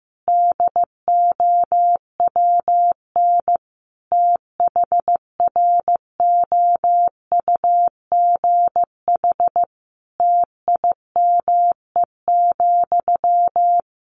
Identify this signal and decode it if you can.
Morse code, heard as a keyed tone, about 15 wpm, 700 Hz. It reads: DOWN THROUGH TIME,
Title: A Sound Of Thunder 15wpm